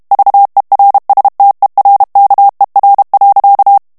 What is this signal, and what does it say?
Morse code, heard as a keyed tone, about 32 wpm, 800 Hz. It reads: VERSTERKER.
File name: HOLLANDAIS texte clair 2 vitesse 30 mots minute